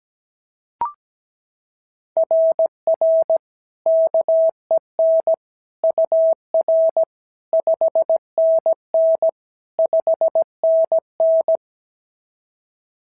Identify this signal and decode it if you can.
Morse code, heard as a keyed tone, about 15 wpm, 650 Hz. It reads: RR KEN UR 5NN 5NN
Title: SOTA QSO Practice 17wpm